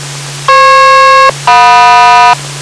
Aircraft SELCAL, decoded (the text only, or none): FP-JM